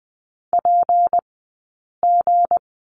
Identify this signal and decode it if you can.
Morse code, heard as a keyed tone, about 20 wpm, spaced as for 8 wpm, 700 Hz. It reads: PG
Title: 171 Lesson 28d - Number 9 - Two Characters 20wpm